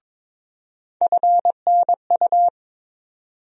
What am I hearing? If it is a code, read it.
Morse code, heard as a keyed tone, about 21 wpm, 700 Hz. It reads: FNU
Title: USA Call Sign Postfixes - 3 Letters 22wpm